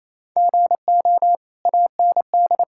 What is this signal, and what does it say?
Morse code, heard as a keyed tone, about 28 wpm, 700 Hz. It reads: GO AND
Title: Sentences from Top 1000 Words - Review Part 5 - 28wpm